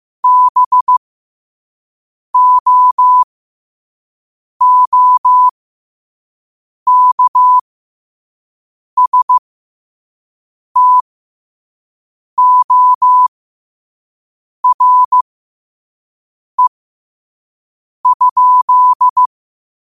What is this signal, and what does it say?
Morse code, heard as a keyed tone, about 15 wpm, spaced as for 5 wpm, 1000 Hz. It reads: BOOKSTORE?